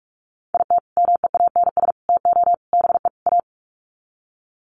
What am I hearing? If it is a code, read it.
Morse code, heard as a keyed tone, about 44 wpm, 700 Hz. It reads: IT MEANS TO BE A
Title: Sets of 5 Words Encore2 45wpm